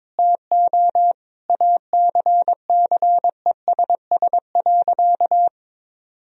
Morse code, heard as a keyed tone, about 21 wpm, 700 Hz. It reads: TO ACCESS.
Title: Sentences from Top 1000 Words - Review Part 4 - 22wpm